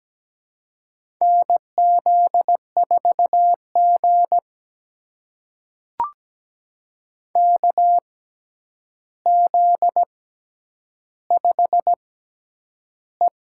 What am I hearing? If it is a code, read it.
Morse code, heard as a keyed tone, about 17 wpm, spaced as for 9 wpm, 700 Hz. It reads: NZ4G KZ5E
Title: USA Call Signs - 4 Characters - 6x Character Spacing - 17wpm